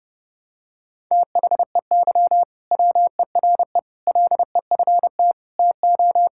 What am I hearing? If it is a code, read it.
Morse code, heard as a keyed tone, about 32 wpm, 700 Hz. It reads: THEY WERE LEFT TO
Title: Sentences from Top 1000 Words - Review Part 5 - 30wpm